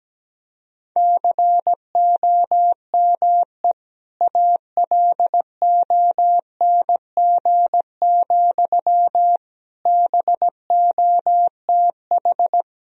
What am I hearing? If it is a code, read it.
Morse code, heard as a keyed tone, about 17 wpm, 700 Hz. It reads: COME ALONG, BOTH
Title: Sentences from Top 1000 Words - Review Part 1 - 17wpm